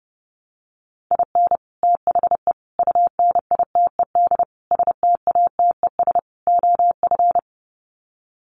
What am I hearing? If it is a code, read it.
Morse code, heard as a keyed tone, about 30 wpm, 700 Hz. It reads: IN THE UNITED STATES OF
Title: Sets of 5 Words Encore2 30wpm